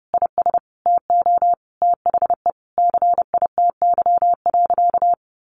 Morse code, heard as a keyed tone, about 28 wpm, 700 Hz. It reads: IS TO THE CITY.
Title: Sentences from Top 1200 Words 30wpm